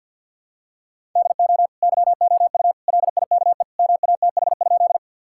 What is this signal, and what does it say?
Morse code, heard as a keyed tone, about 50 wpm, 700 Hz. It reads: DO YOU LIKE MATH?
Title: Sentences from Top 1000 Words - Review Part 6 - 50wpm